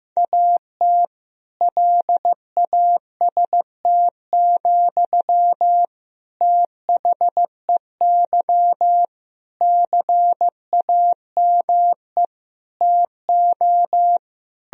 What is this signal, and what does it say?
Morse code, heard as a keyed tone, about 15 wpm, 700 Hz. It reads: AT LAST, THEY CAME TO
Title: Sentences from Top 1000 Words - Review Part 1 - 15wpm